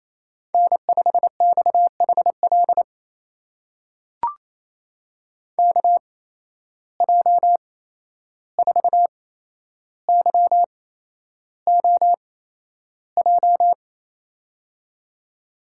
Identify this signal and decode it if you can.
Morse code, heard as a keyed tone, about 28 wpm, spaced as for 11 wpm, 700 Hz. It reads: N5XHL KJ4YOJ
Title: USA Call Signs - 8x Character Spacing - 28wpm